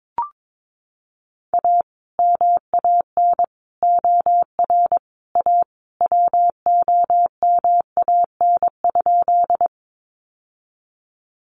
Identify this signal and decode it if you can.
Morse code, heard as a keyed tone, about 22 wpm, 700 Hz. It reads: A MAN OR A WOMAN?
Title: Sentences from Top 1100 Words 22wpm